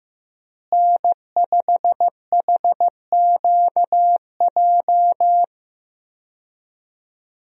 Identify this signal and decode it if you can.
Morse code, heard as a keyed tone, about 15 wpm, 700 Hz. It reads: N5HQJ
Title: USA Call Signs - 5 Characters - Repeated 3x - 15wpm